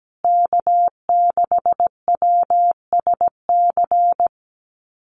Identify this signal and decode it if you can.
Morse code, heard as a keyed tone, about 17 wpm, 700 Hz. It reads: K6WSC
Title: USA Call Signs Repeated 2x 17wpm